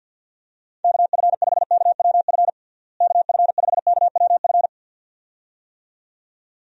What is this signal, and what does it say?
Morse code, heard as a keyed tone, about 48 wpm, 700 Hz. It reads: KF5KWF KF5KWF
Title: USA Call Signs - 6 Characters - Repeated 2x - 50wpm